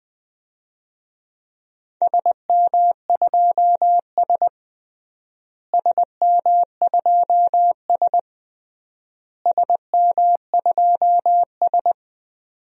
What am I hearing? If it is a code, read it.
Morse code, heard as a keyed tone, about 20 wpm, 700 Hz. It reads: SM2S SM2S SM2S
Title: DXCC-100 Call Signs - 4 Characters - Repeated 3x - 20wpm